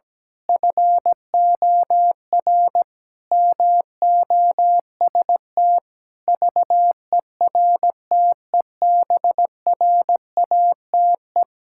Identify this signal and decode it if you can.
Morse code, heard as a keyed tone, about 17 wpm, 700 Hz. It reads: FOR MOST VERTEBRATE
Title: Pangram Challenge v2 - 17wpm